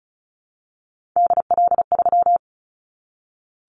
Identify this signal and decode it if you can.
Morse code, heard as a keyed tone, about 34 wpm, 700 Hz. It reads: DL3
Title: Callsign Prefixes for Top 50 DXCC - 35wpm